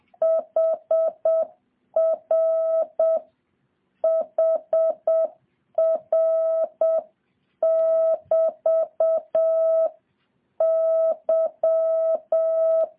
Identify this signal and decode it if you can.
Morse code, heard as a keyed tone, about 7 wpm, 650 Hz. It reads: HR HR <BT> Y
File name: jh8cbh / 2023cw